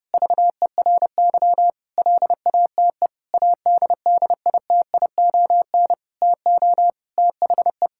Morse code, heard as a keyed tone, about 30 wpm, 700 Hz. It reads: VERY LATE ADDITION TO THE
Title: Sentences from Top 1000 Words - Review Part 6 - 30wpm